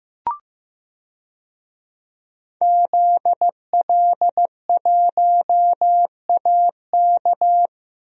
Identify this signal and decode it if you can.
Morse code, heard as a keyed tone, about 15 wpm, 700 Hz. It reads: ZL1AK
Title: DXCC-100 Call Signs - 5 Characters - Repeated 3x - 15wpm